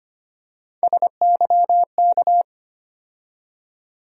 Morse code, heard as a keyed tone, about 25 wpm, 700 Hz. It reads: SYK
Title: USA Call Sign Postfixes - 3 Letters 25wpm